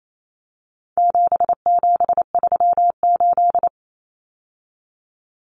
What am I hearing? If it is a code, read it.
Morse code, heard as a keyed tone, about 28 wpm, 700 Hz. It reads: 7738
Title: Numbers - 4-Digit 28wpm